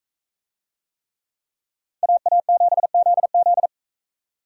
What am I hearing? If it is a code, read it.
Morse code, heard as a keyed tone, about 39 wpm, 700 Hz. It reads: AA7ZZ